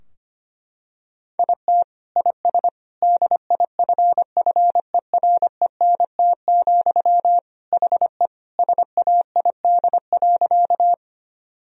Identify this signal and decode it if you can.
Morse code, heard as a keyed tone, about 25 wpm, 700 Hz. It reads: IT IS DIFFERENT, HE SAID.
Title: Sentences from Top 1000 Words - Review Part 3 - 25wpm